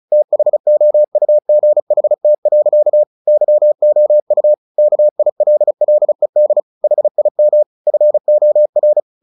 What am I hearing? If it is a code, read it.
Morse code, heard as a keyed tone, about 34 wpm, 600 Hz. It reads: THOUGHT. YOU KILLED HIM FOR
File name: the old man and the sea 2 35wpm